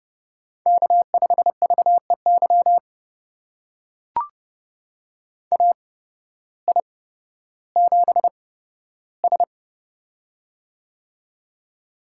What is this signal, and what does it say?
Morse code, heard as a keyed tone, about 30 wpm, spaced as for 14 wpm, 700 Hz. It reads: K5VEY AI7S
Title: USA Call Signs - 8x Character Spacing - 30wpm